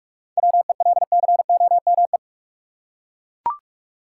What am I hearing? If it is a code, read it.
Morse code, heard as a keyed tone, about 41 wpm, 700 Hz. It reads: WELCOME
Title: Seven Letter Words - 45wpm